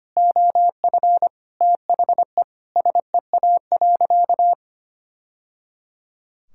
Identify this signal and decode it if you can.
Morse code, heard as a keyed tone, about 25 wpm, 700 Hz. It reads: OF THE SEA.
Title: Sentences from Top 1000 Words - Review Part 6 - 25wpm